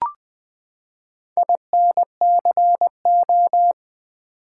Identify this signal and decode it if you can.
Morse code, heard as a keyed tone, about 20 wpm, 700 Hz. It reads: INCO
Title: Build A Word - 4 Letters At A Time - 11-Letter Words - 20wpm